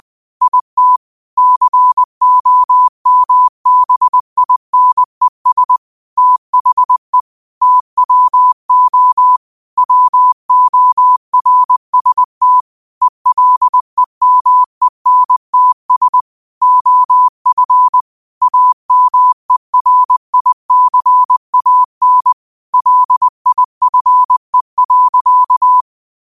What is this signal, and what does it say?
Morse code, heard as a keyed tone, about 20 wpm, 1000 Hz. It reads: IT COMBINES THE TWO WORST ELEMENTS OF AMERICAN LIFE.
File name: qotdfiles / 20wpm